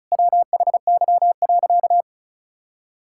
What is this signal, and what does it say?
Morse code, heard as a keyed tone, about 34 wpm, 700 Hz. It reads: WHY.
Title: Sentences from Top 1000 Words - Review Part 6 - 35wpm